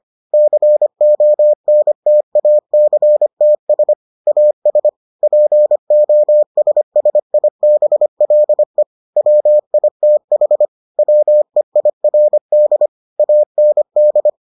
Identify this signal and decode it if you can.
Morse code, heard as a keyed tone, about 25 wpm, 600 Hz. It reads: CONTACTS AS POSSIBLE WITH WEIRD AND